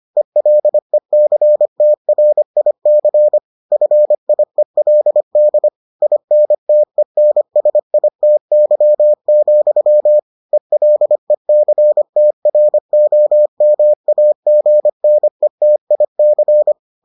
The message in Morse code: ELECTRIC FIELD INTENSITY, ELECTROMAGNETIC

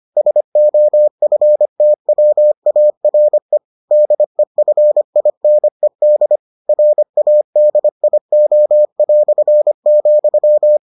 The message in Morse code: SOFTWARE DEFINED RADIO",

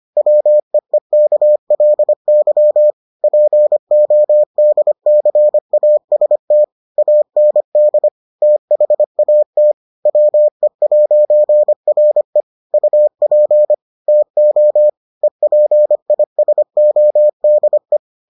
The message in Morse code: WEEKLY PODCAST AND THAT WE'RE UP TO EPISODE

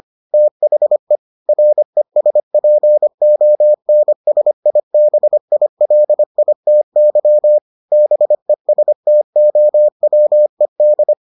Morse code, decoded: THE RESPONSIBILITY BESTOWED

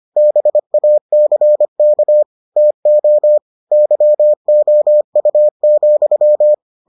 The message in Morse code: BACK TO YOU,

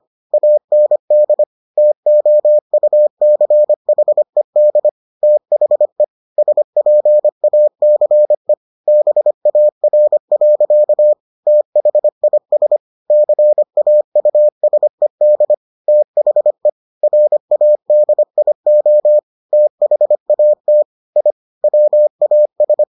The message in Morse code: AND TOUCHED THE SPACE BAR. THIS CAUSED THE RADIO THAT I WAS

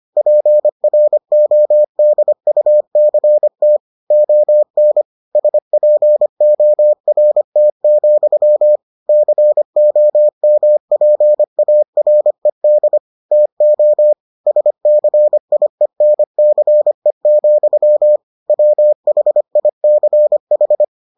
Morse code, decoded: PRODUCT ON SPORT, COMPARED TO SCIENCE, WHICH

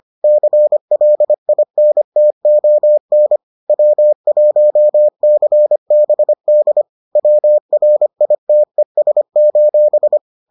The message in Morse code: CLINTON W1CBD WRITES:

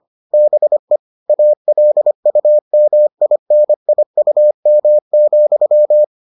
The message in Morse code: BE ALUMINIUM,

